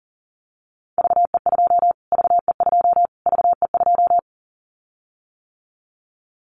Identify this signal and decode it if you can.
Morse code, heard as a keyed tone, about 37 wpm, 700 Hz. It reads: VE2 VE2 VE2
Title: Callsign Prefixes for Top 25 DXCC - Repeated 3x - 40wpm